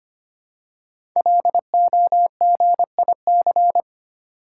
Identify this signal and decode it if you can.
Morse code, heard as a keyed tone, about 25 wpm, 700 Hz. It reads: LOGIC
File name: 5 Letter Words - First 3 Letters Given - 25wpm